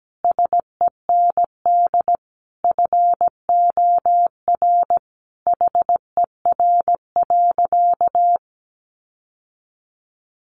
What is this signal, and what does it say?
Morse code, heard as a keyed tone, about 17 wpm, 700 Hz. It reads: SEND FOR HER.
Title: Sentences from Top 1500 Words 17wpm